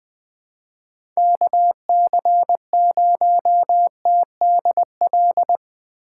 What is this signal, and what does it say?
Morse code, heard as a keyed tone, about 21 wpm, 700 Hz. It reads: KC0TDL
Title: USA Call Signs - 6 Characters - 2x Character Spacing - 20wpm